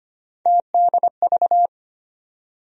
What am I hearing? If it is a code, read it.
Morse code, heard as a keyed tone, about 25 wpm, 700 Hz. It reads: TDV
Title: USA Call Sign Postfixes - 3 Letters 25wpm